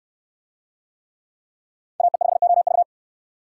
Morse code, heard as a keyed tone, about 60 wpm, 700 Hz. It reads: AE5OV